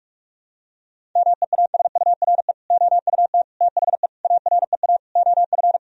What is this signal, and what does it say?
Morse code, heard as a keyed tone, about 46 wpm, 700 Hz. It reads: MEASURE OUT THE AREA OF
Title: Sentences from Top 1000 Words - Review Part 2 - 45wpm